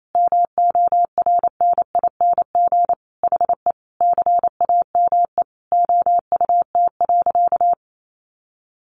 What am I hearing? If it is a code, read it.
Morse code, heard as a keyed tone, about 28 wpm, 700 Hz. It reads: MORNING HE CAME OUT.
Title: Sentences from Top 1300 Words 28wpm